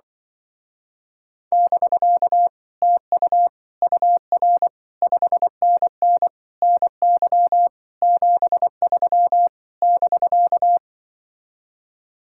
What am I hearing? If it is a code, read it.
Morse code, heard as a keyed tone, about 24 wpm, 700 Hz. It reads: <BK> TU UR 5NN NY 73 <BK>
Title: Speed-Racing Common QSO Phrases - 20wpm